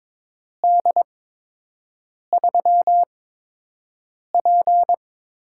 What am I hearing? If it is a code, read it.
Morse code, heard as a keyed tone, about 22 wpm, spaced as for 6 wpm, 700 Hz. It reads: D3P
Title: USA Call Signs - 4 Characters - 8x Character Spacing - 22wpm